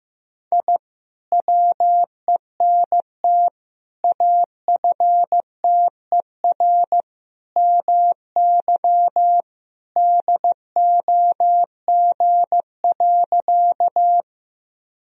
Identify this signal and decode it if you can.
Morse code, heard as a keyed tone, about 15 wpm, 700 Hz. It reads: I WENT AFTER MY DOG.
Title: Sentences from Top 1000 Words - Review Part 2 - 15wpm